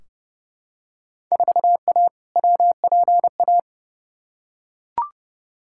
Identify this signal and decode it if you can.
Morse code, heard as a keyed tone, about 31 wpm, 700 Hz. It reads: 4A WPA
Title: Field Day Exchanges Encore - 30wpm